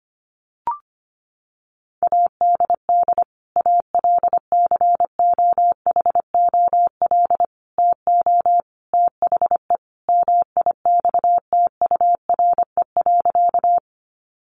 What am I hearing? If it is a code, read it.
Morse code, heard as a keyed tone, about 25 wpm, 700 Hz. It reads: ADD ALCOHOL TO THE MIXTURE.
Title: Sentences from Top 1500 Words 25wpm